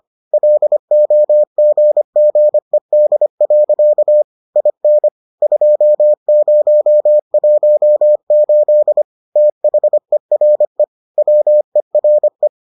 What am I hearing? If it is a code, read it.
Morse code, heard as a keyed tone, about 25 wpm, 600 Hz. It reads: LOGGED. IN 2018 THERE WERE